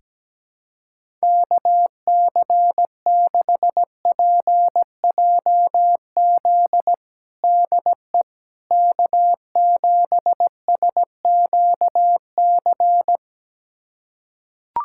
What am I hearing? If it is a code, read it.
Morse code, heard as a keyed tone, about 17 wpm, 700 Hz. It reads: KC6PJZ DE K7SQC
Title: USA Callsign DE Callsign 17wpm